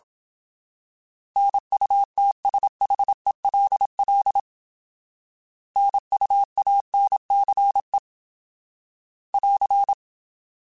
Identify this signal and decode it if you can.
Morse code, heard as a keyed tone, about 27 wpm, 800 Hz. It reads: NUTSHELL NUANCE <AR>
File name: Alphabetical / Alphabetical mp3s / Ns @25